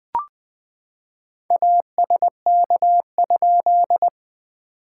Morse code, heard as a keyed tone, about 20 wpm, 700 Hz. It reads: ASK?
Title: SOTA QSO Practice 20wpm